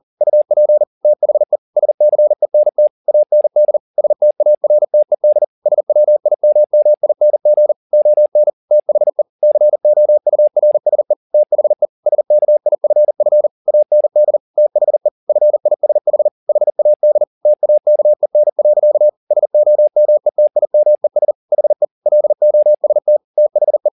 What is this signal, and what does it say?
Morse code, heard as a keyed tone, about 40 wpm, 600 Hz. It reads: UP THE SCENT AND STARTED SWIMMING ON THE COURSE THE SKIFF AND THE FISH HAD TAKEN. SOMETIMES HE LOST THE
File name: the old man and the sea 2 40wpm